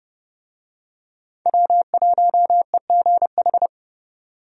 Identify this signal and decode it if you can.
Morse code, heard as a keyed tone, about 30 wpm, 700 Hz. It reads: W1EGH
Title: USA Call Signs - 5 Characters - 30wpm